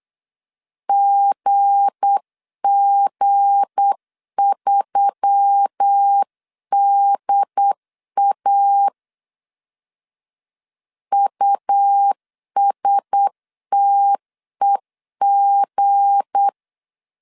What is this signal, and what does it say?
Morse code, heard as a keyed tone, about 8 wpm, 800 Hz. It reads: GG3DA USTEG
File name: LEZIONI / 7 GF9Z / 7 G 8wpm 3118